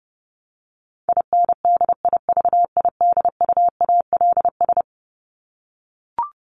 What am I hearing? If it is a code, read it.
Morse code, heard as a keyed tone, about 28 wpm, 700 Hz. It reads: INDIVIDUALS
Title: ES:S Suffix - Top 100 Words with Base Words 30wpm